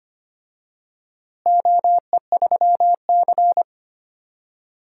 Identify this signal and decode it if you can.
Morse code, heard as a keyed tone, about 25 wpm, 700 Hz. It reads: OE3C
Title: DXCC-100 Call Signs - 4 Characters - Repeated 3x - 25wpm